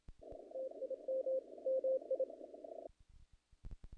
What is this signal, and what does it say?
Morse code, heard as a keyed tone, about 27 wpm, 530 Hz. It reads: TIM MI
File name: contests / naqp-cw-k5tr-op-k5pi- -1800